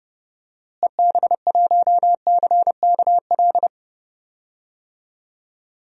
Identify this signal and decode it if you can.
Morse code, heard as a keyed tone, about 30 wpm, 700 Hz. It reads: EB1CKL
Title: DXCC-100 Call Signs - 6 Characters - Repeated 2x - 30wpm